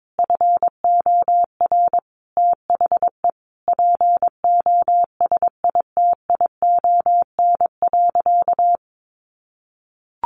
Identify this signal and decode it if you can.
Morse code, heard as a keyed tone, about 22 wpm, 700 Hz. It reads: FOR THE POSITION.
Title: Sentences from Top 1400 Words 22wpm